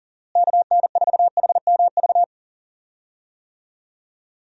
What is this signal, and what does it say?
Morse code, heard as a keyed tone, about 40 wpm, 700 Hz. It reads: KN4HMV
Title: USA Call Signs - 6 Characters - Repeated 3x - 40wpm